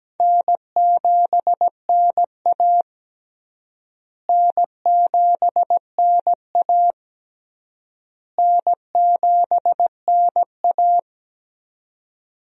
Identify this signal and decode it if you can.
Morse code, heard as a keyed tone, about 17 wpm, 700 Hz. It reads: N7NA N7NA N7NA
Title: USA Call Signs - 4 Characters - Repeated 3x - 17wpm